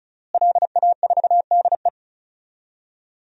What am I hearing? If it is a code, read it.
Morse code, heard as a keyed tone, about 36 wpm, 700 Hz. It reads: LA4DE
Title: DXCC-100 Call Signs - 5 Characters - Repeated 2x - 35wpm